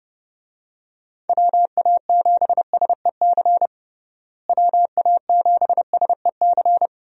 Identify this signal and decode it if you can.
Morse code, heard as a keyed tone, about 29 wpm, 700 Hz. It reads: WA7SEC WA7SEC
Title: USA Call Signs - 6 Characters - Repeated 3x - 30wpm